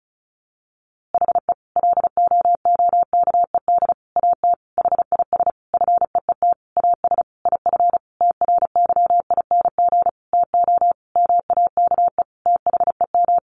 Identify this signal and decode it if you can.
Morse code, heard as a keyed tone, about 34 wpm, 700 Hz. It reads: HE LOOKED AT HIS FEET AS IF TRYING TO MAKE THEM
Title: A Sound Of Thunder 35wpm